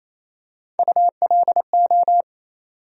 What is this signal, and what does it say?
Morse code, heard as a keyed tone, about 28 wpm, 700 Hz. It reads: ULO
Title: USA Call Sign Postfixes - 3 Letters 28wpm